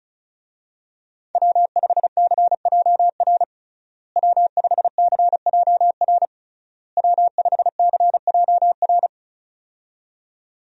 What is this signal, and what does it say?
Morse code, heard as a keyed tone, about 34 wpm, 700 Hz. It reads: W5CJR W5CJR W5CJR
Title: USA Call Signs - 5 Characters - Repeated 3x - 35wpm